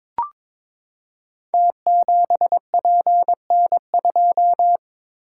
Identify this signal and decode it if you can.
Morse code, heard as a keyed tone, about 22 wpm, 690 Hz. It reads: T7PN2
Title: Code Groups - 5 Letter:Numbers - 22wpm